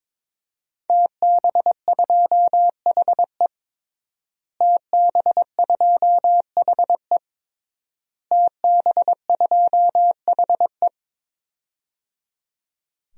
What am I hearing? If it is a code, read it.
Morse code, heard as a keyed tone, about 22 wpm, 700 Hz. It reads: TB2HE TB2HE TB2HE
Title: DXCC-100 Call Signs - 5 Characters - Repeated 3x - 22wpm